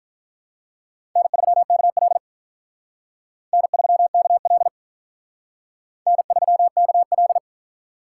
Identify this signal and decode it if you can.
Morse code, heard as a keyed tone, about 42 wpm, 700 Hz. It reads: N3KL N3KL N3KL
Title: Speed Racing - USA 1x2 Call Signs 35wpm